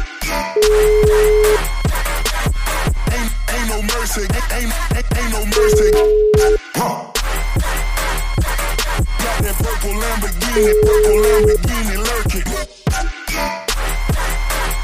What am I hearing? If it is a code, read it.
Ringback tone (Europe, 425 Hz), with noise.